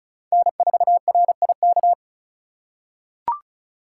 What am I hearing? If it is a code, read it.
Morse code, heard as a keyed tone, about 36 wpm, 700 Hz. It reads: N4RIK
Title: USA Call Signs - 5 Characters - 4x Character Spacing - 35wpm